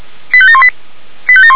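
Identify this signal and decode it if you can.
Five-tone selective call (ZVEI), heard as quick runs of five tones, tones of about 70 ms. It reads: 85718
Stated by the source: ZVEI five-tone call received by SDR, from the feuerwehr-orlamuende.org fire brigade audio server.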